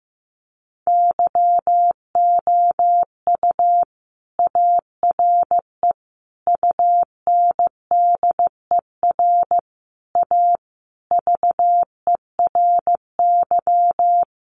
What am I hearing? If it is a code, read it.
Morse code, heard as a keyed tone, about 15 wpm, 700 Hz. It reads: YOU ARE UNDER A VERY
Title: Sentences from Top 1200 Words 15wpm